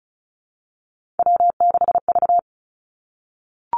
Morse code, heard as a keyed tone, about 36 wpm, 700 Hz. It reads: W6V
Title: Three Challenging Letter-Number Combinations 35wpm